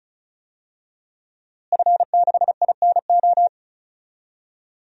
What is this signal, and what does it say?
Morse code, heard as a keyed tone, about 36 wpm, 700 Hz. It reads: F6INO
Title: DXCC-100 Call Signs - 5 Characters - Repeated 2x - 35wpm